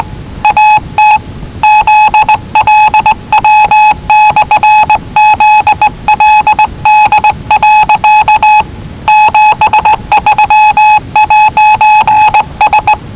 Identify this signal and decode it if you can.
Morse code, heard as a keyed tone, about 19 wpm, 850 Hz. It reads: AT ZLW/ZLD. 73'S
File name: WellingtonRadioZlwCloseDownOnHfAndMf / 01-Zlw 4277khz 417.5 qrt until 0925z